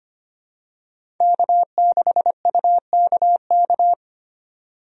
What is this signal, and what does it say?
Morse code, heard as a keyed tone, about 25 wpm, 700 Hz. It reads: K6UKK
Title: USA Call Signs - 5 Characters - 25wpm